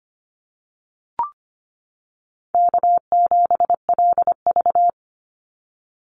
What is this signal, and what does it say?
Morse code, heard as a keyed tone, about 25 wpm, 700 Hz. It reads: K7LV
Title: Easy Call Signs - 25wpm